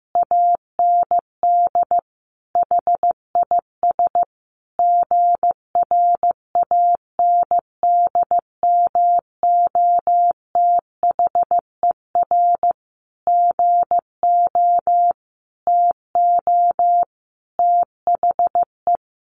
Morse code, heard as a keyed tone, about 15 wpm, 700 Hz. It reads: AND HIS GRANDMOTHER GO TO THE
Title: Additional Easy Phrases 15wpm